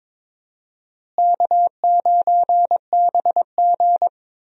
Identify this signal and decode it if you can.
Morse code, heard as a keyed tone, about 22 wpm, 700 Hz. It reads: K9BG
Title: USA Call Signs - 4 Characters - Repeated 2x - 22wpm